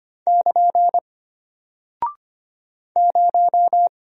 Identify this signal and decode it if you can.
Morse code, heard as a keyed tone, about 25 wpm, 700 Hz. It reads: <KN> 0
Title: Single Character (Letter:Number:Punctuation:Prosign) Alt 25wpm